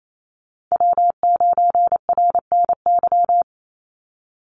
Morse code, heard as a keyed tone, about 28 wpm, 700 Hz. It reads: W9RNY
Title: Call Signs - 28wpm